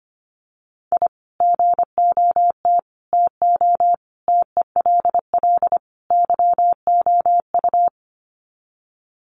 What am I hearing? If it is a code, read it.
Morse code, heard as a keyed tone, about 25 wpm, 700 Hz. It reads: I GOT TO TELL YOU
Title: Sets of 5 Words Encore2 25wpm